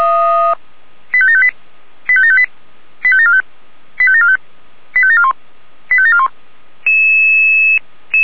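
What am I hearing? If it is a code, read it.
Five-tone selective call (ZVEI), heard as quick runs of five tones, tones of about 70 ms. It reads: 85758 85758 85745 85745 85741 85741
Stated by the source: ZVEI five-tone selective call received on an SDR.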